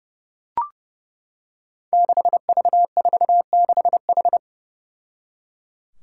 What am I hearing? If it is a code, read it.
Morse code, heard as a keyed tone, about 30 wpm, 700 Hz. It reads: 6V46H
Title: Five Challenging Letter-Number Combinations 30wpm